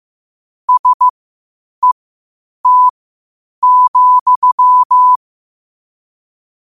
Morse code, heard as a keyed tone, about 15 wpm, spaced as for 8 wpm, 1000 Hz. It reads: SET,
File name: qotdfiles / 8wpm